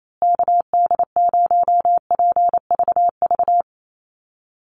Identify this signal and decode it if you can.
Morse code, heard as a keyed tone, about 28 wpm, 700 Hz. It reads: KD0PVV
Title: USA Call Signs Repeated 3x 28wpm